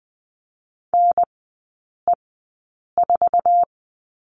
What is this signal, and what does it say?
Morse code, heard as a keyed tone, about 20 wpm, spaced as for 8 wpm, 700 Hz. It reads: NE4
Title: 138 Lesson 23e - Letter f - Three Characters 20wpm